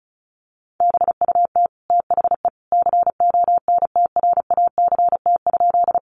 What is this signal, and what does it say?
Morse code, heard as a keyed tone, about 34 wpm, 700 Hz. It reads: BUT THE CONTRACT?
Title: Sentences from Top 1400 Words 35wpm